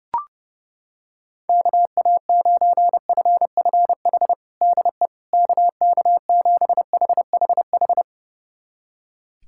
Morse code, heard as a keyed tone, about 30 wpm, 700 Hz. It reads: KA9FFH DE KK7HHH
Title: USA Callsign DE Callsign - 6 Characters - 30wpm